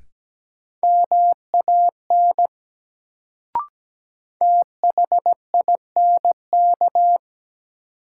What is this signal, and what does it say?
Morse code, heard as a keyed tone, about 17 wpm, 700 Hz. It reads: MAN THINK
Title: Sentences from Top 200 Words - New Word Drill - 17wpm